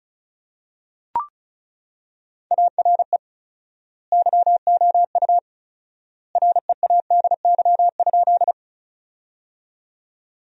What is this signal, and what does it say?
Morse code, heard as a keyed tone, about 34 wpm, 700 Hz. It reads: ARE YOU READY?
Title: 4x Word-Spacing - Sentences from Top 400 Words - 35wpm